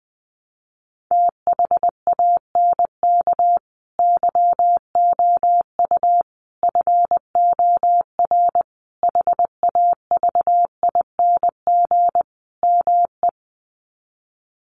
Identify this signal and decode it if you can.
Morse code, heard as a keyed tone, about 20 wpm, 700 Hz. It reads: THANK YOU FOR HAVING ME
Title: Sets of 5 Words Encore 20wpm